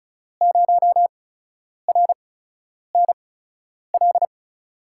Morse code, heard as a keyed tone, about 34 wpm, spaced as for 10 wpm, 700 Hz. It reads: 0RNL
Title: USA Call Signs - 6 Characters - 8x Character Spacing - 35wpm